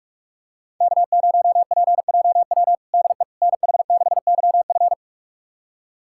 Morse code, heard as a keyed tone, about 46 wpm, 700 Hz. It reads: K0PJW DE NH6YF
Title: USA Callsign DE Callsign - 5 Characters - 45wpm